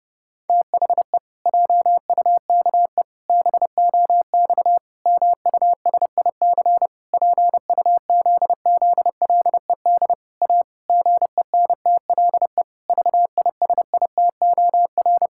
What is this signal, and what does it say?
Morse code, heard as a keyed tone, about 30 wpm, 700 Hz. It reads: THE JUKE BOX MUSIC PUZZLED A GENTLE VISITOR
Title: Pangram Challenge v2 - 30wpm